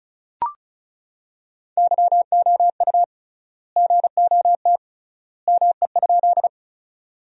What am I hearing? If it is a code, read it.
Morse code, heard as a keyed tone, about 36 wpm, 700 Hz. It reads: YOU GOT ME?
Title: 3x Word-Spacing - Sentences from Top 300 Words - 35wpm